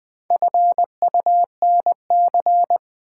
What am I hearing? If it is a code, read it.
Morse code, heard as a keyed tone, about 20 wpm, 700 Hz. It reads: FUNC
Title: Build A Word - 4 Letters At A Time - 10-Letter Words - 20wpm